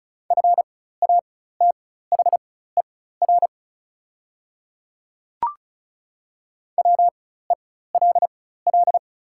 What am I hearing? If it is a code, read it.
Morse code, heard as a keyed tone, about 36 wpm, spaced as for 16 wpm, 700 Hz. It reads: FATHER WELL
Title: 4x Character Spacing - Top 500 Words - 35wpm